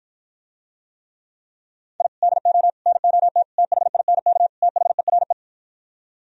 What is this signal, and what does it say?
Morse code, heard as a keyed tone, about 52 wpm, 700 Hz. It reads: I DO NOT THINK THERE